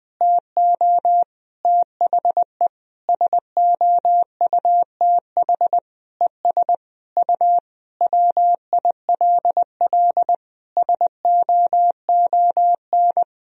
Morse code, heard as a keyed tone, about 20 wpm, 700 Hz. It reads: TO THE SOUTH ES U WILL SOON
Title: Sentences from Top 400 Words and Abbreviations 20wpm